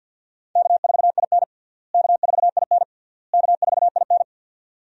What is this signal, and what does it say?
Morse code, heard as a keyed tone, about 52 wpm, 700 Hz. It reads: K4IN K4IN K4IN
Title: USA Call Signs - 4 Characters - Repeated 3x - 50wpm